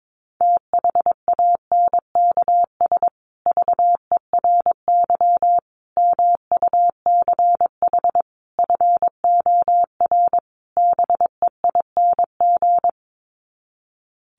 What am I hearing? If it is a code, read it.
Morse code, heard as a keyed tone, about 22 wpm, 700 Hz. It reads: THANKS VERY MUCH FOR BEING
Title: Sets of 5 Words Encore 22wpm